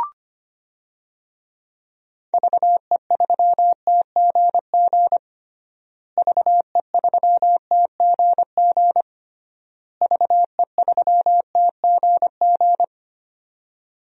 Morse code, heard as a keyed tone, about 25 wpm, 700 Hz. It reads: VE3TGG VE3TGG VE3TGG
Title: DXCC-100 Call Signs - 6 Characters - Repeated 3x - 25wpm